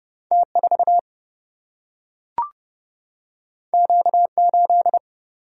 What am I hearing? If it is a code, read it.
Morse code, heard as a keyed tone, about 30 wpm, 700 Hz. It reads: T4 Q8
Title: Single-Letter Single-Number 30wpm